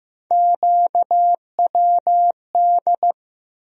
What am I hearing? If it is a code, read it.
Morse code, heard as a keyed tone, about 15 wpm, 700 Hz. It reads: QWD
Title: USA Call Sign Postfixes - 3 Letters 15wpm